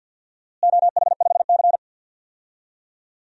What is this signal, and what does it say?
Morse code, heard as a keyed tone, about 48 wpm, 700 Hz. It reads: OH5C
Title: DXCC-100 Call Signs - 4 Characters - Repeated 3x - 50wpm